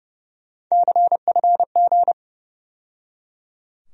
Morse code, heard as a keyed tone, about 30 wpm, 700 Hz. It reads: CFG
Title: USA Call Sign Postfixes - 3 Letters 30wpm